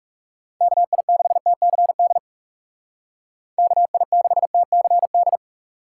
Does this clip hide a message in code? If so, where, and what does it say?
Yes: Morse code from 0.6 to 5.4 s, heard as a keyed tone, about 39 wpm, 700 Hz. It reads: KI6TCD KI6TCD